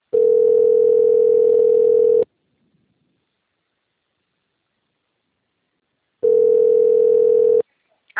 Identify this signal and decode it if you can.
Ringback tone (North America), 440 + 480 Hz.